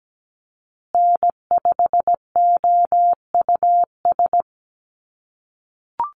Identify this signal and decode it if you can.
Morse code, heard as a keyed tone, about 17 wpm, 700 Hz. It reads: N5OUS
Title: USA Call Signs Repeated 2x 17wpm